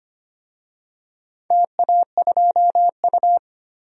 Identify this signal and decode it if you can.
Morse code, heard as a keyed tone, about 25 wpm, 700 Hz. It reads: TA2U
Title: DXCC-100 Call Signs - 4 Characters - 25wpm